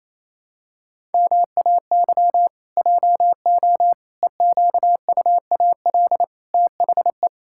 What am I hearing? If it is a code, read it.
Morse code, heard as a keyed tone, about 28 wpm, 700 Hz. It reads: MAY JO EQUAL THE
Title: Pangram Challenge v2 - 28wpm